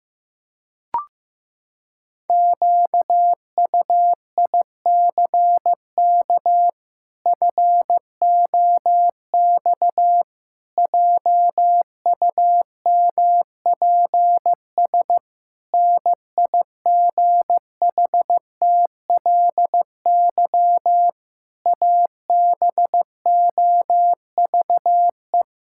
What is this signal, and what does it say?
Morse code, heard as a keyed tone, about 15 wpm, 700 Hz. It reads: QUICK FOX JUMPS NIGHTLY ABOVE
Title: Pangram Challenge v2 - 15wpm